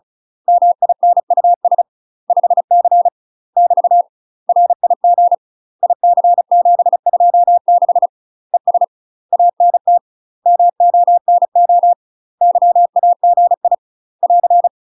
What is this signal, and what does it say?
Morse code, heard as a keyed tone, about 34 wpm, 700 Hz. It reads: MINUS 5C <BT> RIG IC726 ES ANT MONO YAGI <AR>
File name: UK MASTER QSOs TRAINING/35WPM